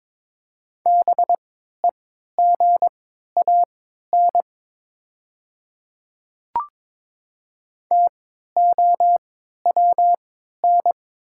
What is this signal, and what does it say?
Morse code, heard as a keyed tone, about 22 wpm, spaced as for 13 wpm, 700 Hz. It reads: BEGAN TOWN
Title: 3x Character Spacing - Top 500 Words - 22wpm